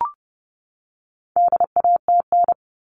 Morse code, heard as a keyed tone, about 30 wpm, 700 Hz. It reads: DATN
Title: All Cut Numbers - 4-Digit 30wpm